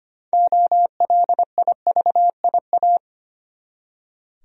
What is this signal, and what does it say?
Morse code, heard as a keyed tone, about 25 wpm, 700 Hz. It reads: OLIVIA
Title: USA Common Names - 25wpm